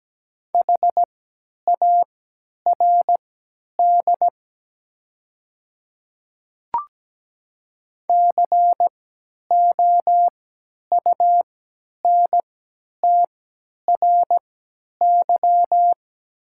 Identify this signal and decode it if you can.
Morse code, heard as a keyed tone, about 17 wpm, spaced as for 10 wpm, 700 Hz. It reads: HARD COUNTRY
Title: 3x Character Spacing - Top 300 Words - 17wpm